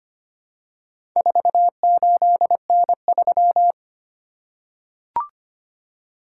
Morse code, heard as a keyed tone, about 25 wpm, 700 Hz. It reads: 48N3
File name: practice / 25wpm / Four Letter-Number Combinations - 25wpm